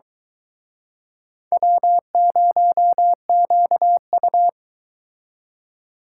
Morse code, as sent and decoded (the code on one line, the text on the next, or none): .-- ----- --.- ..-
W0QU